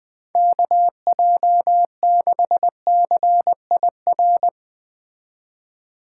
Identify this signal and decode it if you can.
Morse code, heard as a keyed tone, about 20 wpm, 700 Hz. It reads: KJ6CIR